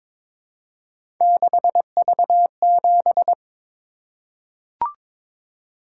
Morse code, heard as a keyed tone, about 22 wpm, 700 Hz. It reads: 6V7
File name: practice / 22wpm / Three Letter-Number Combinations - 22wpm